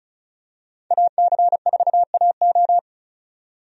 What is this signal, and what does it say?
Morse code, heard as a keyed tone, about 36 wpm, 700 Hz. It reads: AC4AO
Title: USA Call Signs - 5 Characters - 35wpm